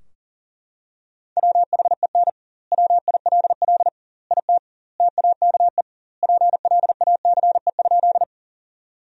Morse code, heard as a keyed tone, about 40 wpm, 700 Hz. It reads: WHEN WILL IT TAKE PLACE?
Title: 2x Word-Spacing - Sentences from Top 200 Words - 40wpm